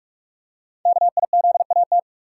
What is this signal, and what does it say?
Morse code, heard as a keyed tone, about 44 wpm, 700 Hz. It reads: KIZAT
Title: Code Groups - 5 Letters - 45wpm